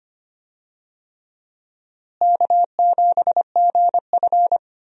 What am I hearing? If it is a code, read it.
Morse code, heard as a keyed tone, about 25 wpm, 700 Hz. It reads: K7GF
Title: USA Call Signs - 4 Characters - Repeated 3x - 25wpm